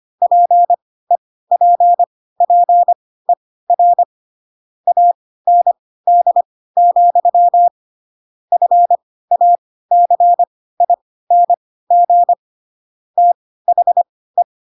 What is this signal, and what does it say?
Morse code, heard as a keyed tone, about 25 wpm, spaced as for 16 wpm, 700 Hz. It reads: PEPPER AND, FACING THE
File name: Pinocchio / Pinocchio ch2 10 25 ik1hge